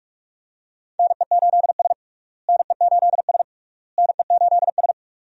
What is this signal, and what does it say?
Morse code, heard as a keyed tone, about 47 wpm, 700 Hz. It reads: NE8S NE8S NE8S
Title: USA Call Signs - 4 Characters - Repeated 3x - 45wpm